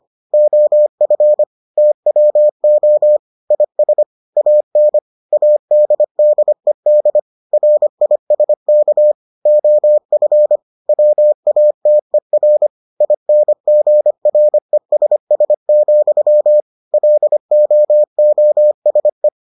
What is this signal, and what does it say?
Morse code, heard as a keyed tone, about 25 wpm, 600 Hz. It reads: OF TWO IS AN ADDED RISK OF WATER INGRESS, LOOSE